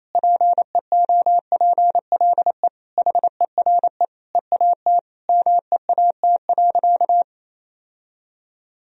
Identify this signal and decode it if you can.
Morse code, heard as a keyed tone, about 28 wpm, 700 Hz. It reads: PEOPLE HERE EAT MEAT.
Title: Sentences from Top 1000 Words - Review Part 3 - 28wpm